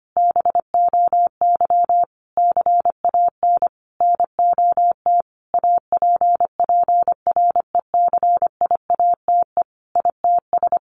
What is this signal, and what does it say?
Morse code, heard as a keyed tone, about 25 wpm, 700 Hz. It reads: BOY CAN NOT APPRECIATE ITS
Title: Sentences from Top 1300 Words 25wpm